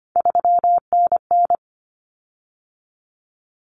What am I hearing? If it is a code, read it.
Morse code, heard as a keyed tone, about 25 wpm, 700 Hz. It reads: 3NN
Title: Common Cut Numbers - 3-Digit 25wpm